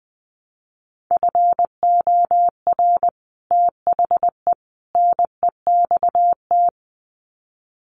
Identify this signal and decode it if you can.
Morse code, heard as a keyed tone, about 20 wpm, 700 Hz. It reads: FOR THE NEXT
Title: Sets of 3 Words 20wpm